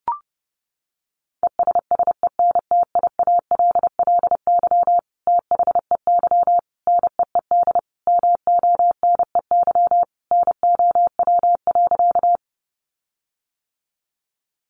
Morse code, heard as a keyed tone, about 28 wpm, 700 Hz. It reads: ESSENTIALLY THEY NEED MONEY NOW.
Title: Sentences from Top 1300 Words 30wpm